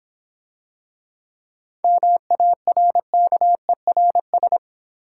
Morse code, heard as a keyed tone, about 26 wpm, 700 Hz. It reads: MARKERS